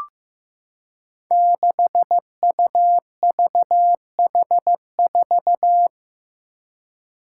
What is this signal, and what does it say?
Morse code, heard as a keyed tone, about 15 wpm, 700 Hz. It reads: 6UVH4
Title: Five Challenging Letter-Number Combinations 15wpm